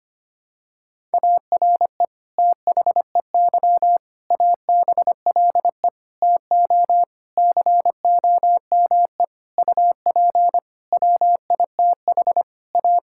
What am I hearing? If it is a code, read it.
Morse code, heard as a keyed tone, about 25 wpm, 700 Hz. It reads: ARE THEY ABLE TO COME UP WITH A
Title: Sentences from Top 1000 Words - Review Part 5 - 25wpm